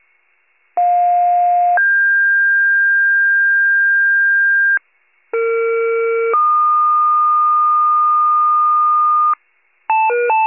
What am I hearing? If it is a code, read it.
Two-tone page, heard as a one-second tone, then a three-second tone, about 710 then 1600 Hz, about 460 then 1220 Hz, with noise. It is heard from 0.8 to 9.3 s.
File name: pager/41 FIRE